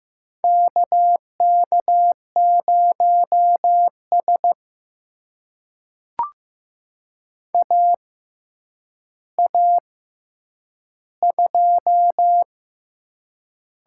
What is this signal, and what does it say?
Morse code, heard as a keyed tone, about 15 wpm, spaced as for 8 wpm, 700 Hz. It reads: KK0S AA2
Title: USA Call Signs - 4 Characters - 6x Character Spacing - 15wpm